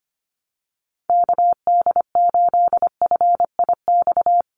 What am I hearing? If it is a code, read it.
Morse code, heard as a keyed tone, about 25 wpm, 700 Hz. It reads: KD8FIX
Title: USA Call Signs 25wpm